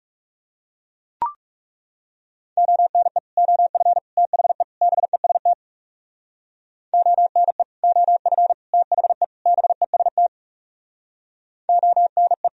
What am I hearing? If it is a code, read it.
Morse code, heard as a keyed tone, about 38 wpm, 700 Hz. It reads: ONE OF THE BEST ONE OF THE BEST ONE